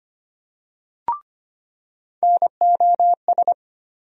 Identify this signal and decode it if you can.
Morse code, heard as a keyed tone, about 25 wpm, 700 Hz. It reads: NOS
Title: USA Call Sign Postfixes - 3 Letters 25wpm